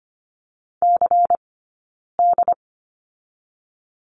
Morse code, heard as a keyed tone, about 25 wpm, 700 Hz. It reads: C D
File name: course / 25:10wpm / 092 Lesson 16d - Letter c - Two Characters 25wpm